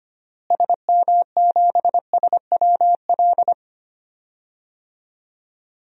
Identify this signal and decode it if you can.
Morse code, heard as a keyed tone, about 25 wpm, 700 Hz. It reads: SM7SWL
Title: DXCC-100 Call Signs - 6 Characters - Repeated 2x - 25wpm